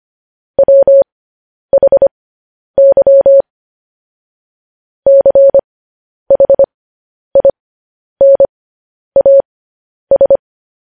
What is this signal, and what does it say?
Morse code, heard as a keyed tone, about 25 wpm, spaced as for 10 wpm, 550 Hz. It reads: WHY CHINAS
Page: news-headlines-in-morse-code-at-10-wpm